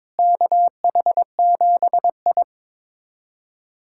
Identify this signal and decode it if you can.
Morse code, heard as a keyed tone, about 22 wpm, 700 Hz. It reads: KH7I
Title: USA Callsign DE Callsign - 4 Characters - 22wpm